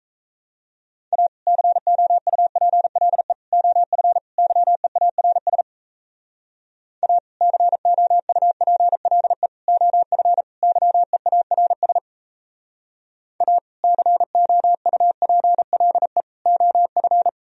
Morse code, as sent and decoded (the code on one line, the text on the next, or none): .- / -.-. --- ..- .--. .-.. / --- ..-. / -.-- . .- .-. ... / .- / -.-. --- ..- .--. .-.. / --- ..-. / -.-- . .- .-. ... / .- / -.-. --- ..- .--. .-.. / --- ..-.
A COUPLE OF YEARS A COUPLE OF YEARS A COUPLE OF